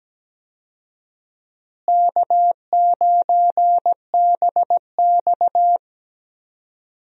Morse code, heard as a keyed tone, about 17 wpm, 700 Hz. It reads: K9BX
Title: USA Call Signs - 4 Characters - Repeated 2x - 17wpm